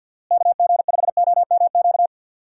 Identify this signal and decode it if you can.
Morse code, heard as a keyed tone, about 49 wpm, 700 Hz. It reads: KG5OMQ
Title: USA Call Signs - 6 Characters - Repeated 2x - 50wpm